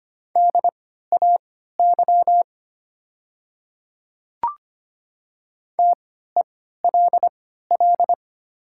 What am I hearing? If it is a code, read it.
Morse code, heard as a keyed tone, about 25 wpm, spaced as for 14 wpm, 700 Hz. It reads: DAY TELL
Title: 3x Character Spacing - Top 200 Words - 25wpm